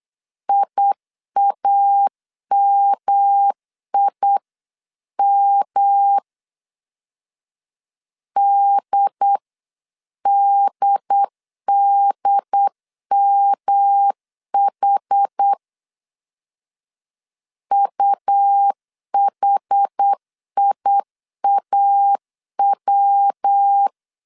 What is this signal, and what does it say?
Morse code, heard as a keyed tone, about 8 wpm, 800 Hz. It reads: IAMIM DDDMH UHIAW
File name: CW / DET / 1 3 1500C S C 8wpm 4214